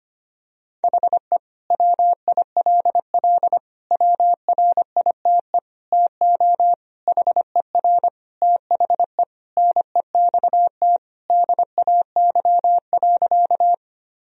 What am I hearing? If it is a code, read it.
Morse code, heard as a keyed tone, about 25 wpm, 700 Hz. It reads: HE WILL WRITE TO HER THE NEXT DAY.
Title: Sentences from Top 1000 Words - Review Part 1 - 25wpm